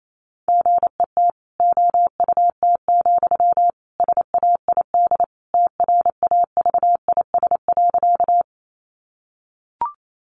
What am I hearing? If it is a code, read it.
Morse code, heard as a keyed tone, about 28 wpm, 700 Hz. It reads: GET OUT, SAID TRAVIS.
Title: A Sound Of Thunder 28wpm